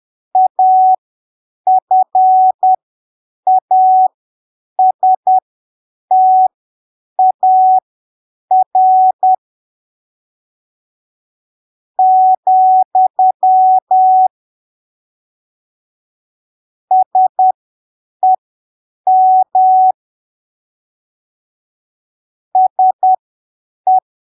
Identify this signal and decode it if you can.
Morse code, heard as a keyed tone, about 10 wpm, spaced as for 7 wpm, 750 Hz. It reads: AFASTAR , SEM SE
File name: arquivos / texto / texto 21 10ppm